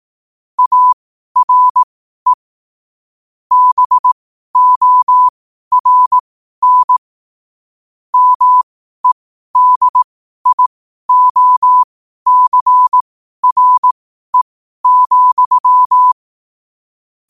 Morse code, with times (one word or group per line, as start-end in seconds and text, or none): 0.6-2.3 ARE
3.5-7.0 BORN
8.1-16.1 MEDIOCRE,